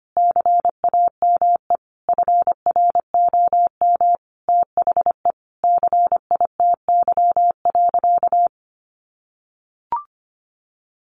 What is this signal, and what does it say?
Morse code, heard as a keyed tone, about 25 wpm, 700 Hz. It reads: CAME FROM THE CITY.
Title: Sentences from Top 1500 Words 25wpm